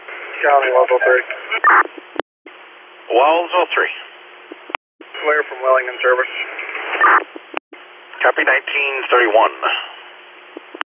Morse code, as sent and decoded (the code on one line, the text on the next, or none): none